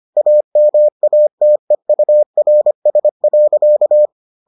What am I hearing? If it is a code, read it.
Morse code, heard as a keyed tone, about 25 wpm, 600 Hz. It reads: AMATEURS.